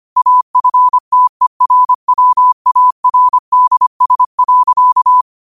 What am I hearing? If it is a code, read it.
Morse code, heard as a keyed tone, about 25 wpm, 1000 Hz. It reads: AFTERWARDS.
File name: qotdfiles / 25wpm